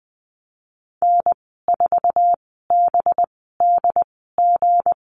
Morse code, heard as a keyed tone, about 20 wpm, spaced as for 15 wpm, 700 Hz. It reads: N4BDG
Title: 231 Lesson 37g - Letter x - Call Signs 20wpm